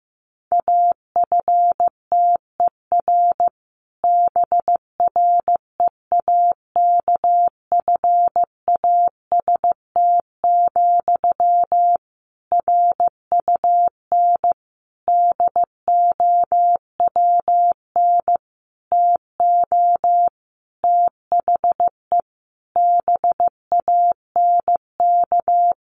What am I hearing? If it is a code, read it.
Morse code, heard as a keyed tone, about 15 wpm, 700 Hz. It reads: AFTER BREAKFAST, RUN DOWN TO THE BANK
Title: Sentences from Top 1500 Words 15wpm